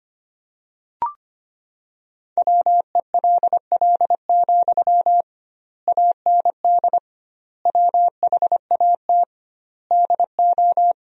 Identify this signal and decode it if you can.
Morse code, heard as a keyed tone, about 25 wpm, 700 Hz. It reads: WELL, AND WHAT DO
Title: 2x Word-Spacing - Sentences from Top 200 Words - 25wpm